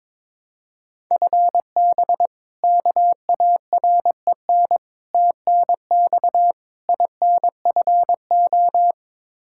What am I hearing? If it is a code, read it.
Morse code, heard as a keyed tone, about 21 wpm, 700 Hz. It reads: FB KAREN TNX INFO
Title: K1USN SST QSO Practice 22wpm